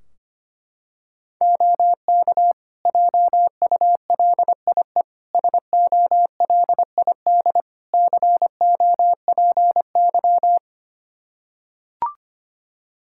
Morse code, heard as a keyed tone, about 25 wpm, 700 Hz. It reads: OK JULIE SOLID COPY
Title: Practice QSO - Exchange 3 - More Info - 25wpm